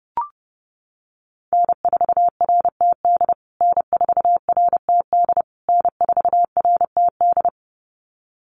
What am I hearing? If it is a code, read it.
Morse code, heard as a keyed tone, about 30 wpm, 700 Hz. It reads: N4RTD N4RTD N4RTD
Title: USA Call Signs Repeated 3x 30wpm